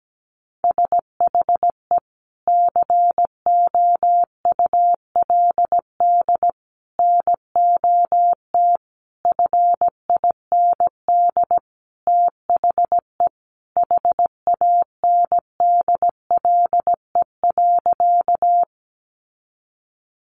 Morse code, heard as a keyed tone, about 17 wpm, 700 Hz. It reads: SHE COULD NOT FIND THE HANDLE.
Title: Sentences from Top 1100 Words 17wpm